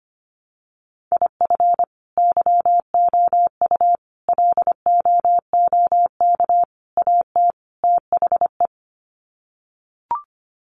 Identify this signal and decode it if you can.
Morse code, heard as a keyed tone, about 25 wpm, 700 Hz. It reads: IF YOU LOOK AT THE
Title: Sets of 5 Words 25wpm